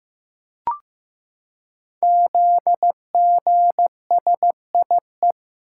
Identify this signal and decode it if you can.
Morse code, heard as a keyed tone, about 15 wpm, 700 Hz. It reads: ZGSIE
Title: Code Groups - 5 Letters - 15wpm